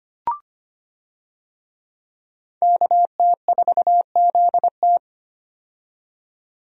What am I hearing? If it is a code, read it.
Morse code, heard as a keyed tone, about 25 wpm, 700 Hz. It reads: KT4ZT
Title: USA Call Signs - 5 Characters - 25wpm